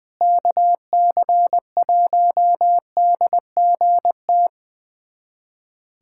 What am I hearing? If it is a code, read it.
Morse code, heard as a keyed tone, about 21 wpm, 700 Hz. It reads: KC1DGT
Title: USA Callsign DE Callsign - 6 Characters - 20wpm